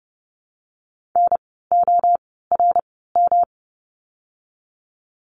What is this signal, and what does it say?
Morse code, heard as a keyed tone, about 30 wpm, spaced as for 17 wpm, 700 Hz. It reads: NORM
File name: course / 30:10wpm / 108 Lesson 18f - Letter m - Words 30wpm